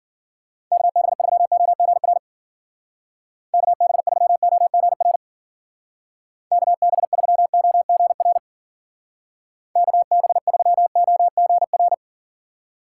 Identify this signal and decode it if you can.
Morse code, heard as a keyed tone, about 49 wpm, 700 Hz. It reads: KB3OGR KB3OGR KB3OGR KB3OGR
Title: Speed Racing - USA 2x3 Call Signs 40wpm